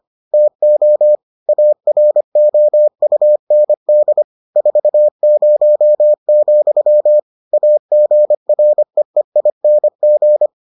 Morse code, heard as a keyed tone, about 25 wpm, 600 Hz. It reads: TO AROUND 40, AGREEING